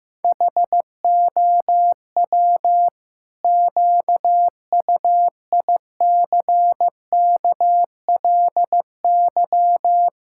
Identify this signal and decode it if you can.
Morse code, heard as a keyed tone, about 15 wpm, 700 Hz. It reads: HOW QUICKLY
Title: Pangram Challenge v2 - 15wpm